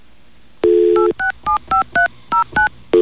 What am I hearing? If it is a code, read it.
DTMF, heard as telephone touch tones, tones of about 120 ms. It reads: 8675309